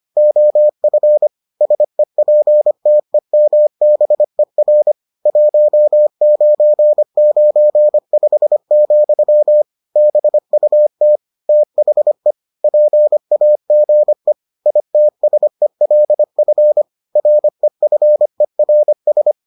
Morse code, heard as a keyed tone, about 25 wpm, 600 Hz. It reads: OF SEPTEMBER 1995, BUT THE PAGE ITSELF REFERS